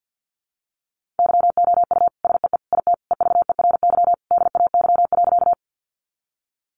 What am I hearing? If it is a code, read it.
Morse code, heard as a keyed tone, about 50 wpm, 700 Hz. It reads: YOU SEE IT EVERY DAY.
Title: Sentences from Top 200 Words 50wpm